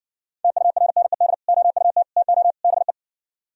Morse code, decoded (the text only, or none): TURNED OUT TO BE